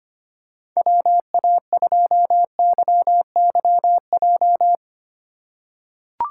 Morse code, heard as a keyed tone, about 25 wpm, 700 Hz. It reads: WA2YYJ
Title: USA Call Signs - 3x Character Spacing - 25wpm